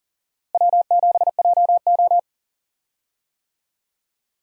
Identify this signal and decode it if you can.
Morse code, heard as a keyed tone, about 37 wpm, 700 Hz. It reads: W7JO
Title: USA Call Signs - 4 Characters - Repeated 2x - 40wpm